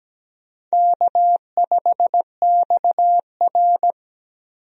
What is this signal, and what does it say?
Morse code, heard as a keyed tone, about 17 wpm, 700 Hz. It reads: K5XR
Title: USA Call Signs - 4 Characters - 17wpm